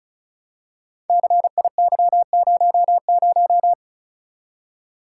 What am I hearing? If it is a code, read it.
Morse code, heard as a keyed tone, about 34 wpm, 700 Hz. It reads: CIY00
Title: Code Groups - 5 Letter:Numbers - 35wpm